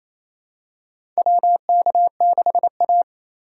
Morse code, heard as a keyed tone, about 28 wpm, 700 Hz. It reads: WK6A
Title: USA Call Signs - 4x Character Spacing - 28wpm